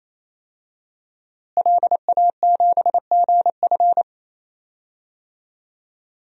Morse code, heard as a keyed tone, about 28 wpm, 700 Hz. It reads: LA7GF
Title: DXCC-100 Call Signs - 5 Characters - Repeated 3x - 28wpm